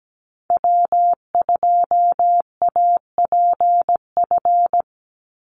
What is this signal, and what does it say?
Morse code, heard as a keyed tone, about 17 wpm, 700 Hz. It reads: W2APF
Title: Call Signs Repeated 2x - 17wpm